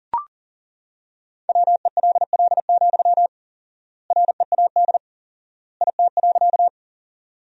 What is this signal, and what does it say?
Morse code, heard as a keyed tone, about 40 wpm, 700 Hz. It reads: WELL, READ IT.
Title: 4x Word-Spacing - Sentences from Top 200 Words - 40wpm